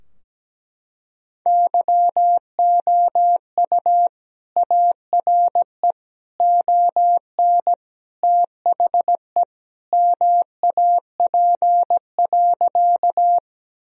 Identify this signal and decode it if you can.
Morse code, heard as a keyed tone, about 17 wpm, 700 Hz. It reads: YOU ARE ON THE MAP.
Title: Sentences from Top 1000 Words - Review Part 2 - 17wpm